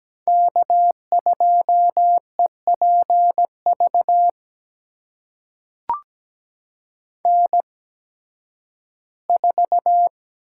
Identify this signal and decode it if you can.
Morse code, heard as a keyed tone, about 17 wpm, spaced as for 11 wpm, 700 Hz. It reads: K2EPV N4
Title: USA Call Signs - 5 Characters - 8x Character Spacing - 17wpm